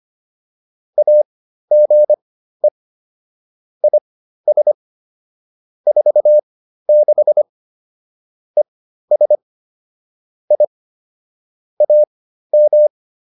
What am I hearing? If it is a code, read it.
Morse code, heard as a keyed tone, about 25 wpm, spaced as for 13 wpm, 600 Hz. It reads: AGE IS 46 ES I AM